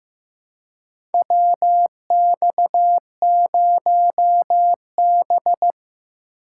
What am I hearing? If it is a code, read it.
Morse code, heard as a keyed tone, about 15 wpm, 700 Hz. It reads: WX0B
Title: USA Call Signs - 4 Characters - 4x Character Spacing - 15wpm